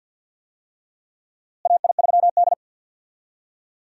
Morse code, heard as a keyed tone, about 51 wpm, 700 Hz. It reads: AI3D
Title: USA Call Signs - 4 Characters - Repeated 3x - 50wpm